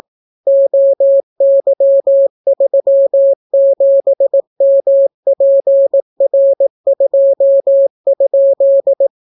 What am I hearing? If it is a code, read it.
Morse code, heard as a keyed tone, about 18 wpm, 550 Hz. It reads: OY37MPR2?